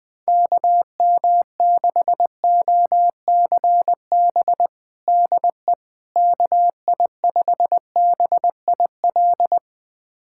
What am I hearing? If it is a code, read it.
Morse code, heard as a keyed tone, about 20 wpm, 700 Hz. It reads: KM6OCB DE KI5BIL
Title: USA Callsign DE Callsign - 6 Characters - 20wpm